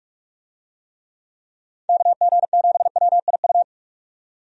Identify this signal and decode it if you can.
Morse code, heard as a keyed tone, about 42 wpm, 700 Hz. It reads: KG7WIU
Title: USA Call Signs - 6 Characters - 45wpm